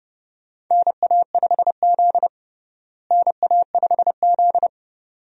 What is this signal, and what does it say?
Morse code, heard as a keyed tone, about 30 wpm, 700 Hz. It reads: NA5Z NA5Z
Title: USA Call Signs - 4 Characters - Repeated 2x - 30wpm